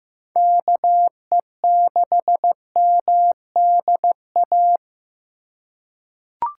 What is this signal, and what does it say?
Morse code, heard as a keyed tone, about 15 wpm, 700 Hz. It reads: KE6MDA
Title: USA Call Signs - 6 Characters - 4x Character Spacing - 15wpm